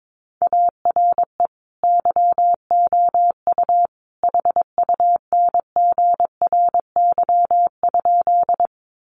Morse code, heard as a keyed tone, about 22 wpm, 700 Hz. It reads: ARE YOU HUNGRY?
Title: Sentences from Top 1400 Words 22wpm